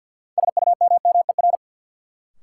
Morse code, heard as a keyed tone, about 52 wpm, 700 Hz. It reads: SUMMER
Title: Sentences from Top 1000 Words - Review Part 6 - New Word Drill - 50wpm